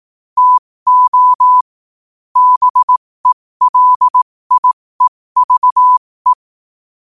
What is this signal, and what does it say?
Morse code, heard as a keyed tone, about 18 wpm, spaced as for 15 wpm, 1000 Hz. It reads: TO BELIEVE
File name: qotdfiles / 15wpm